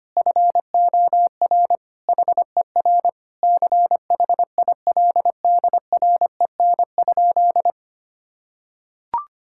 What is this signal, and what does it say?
Morse code, heard as a keyed tone, about 25 wpm, 700 Hz. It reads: FOR HER CHILDREN?
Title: Sentences from Top 1000 Words - Review Part 6 - 25wpm